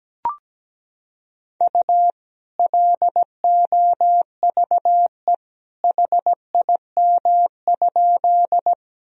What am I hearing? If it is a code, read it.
Morse code, heard as a keyed tone, about 17 wpm, 700 Hz. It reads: U LOVE HIM?
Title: Sentences from Top 400 Words and Abbreviations 17wpm